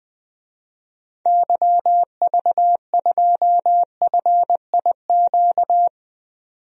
Morse code, heard as a keyed tone, about 20 wpm, 700 Hz. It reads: YV2FIQ
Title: DXCC-100 Call Signs - 6 Characters - 20wpm